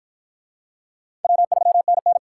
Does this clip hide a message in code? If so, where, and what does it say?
Yes: Morse code from 1.2 to 2.2 s, heard as a keyed tone, about 48 wpm, 700 Hz. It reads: W3NN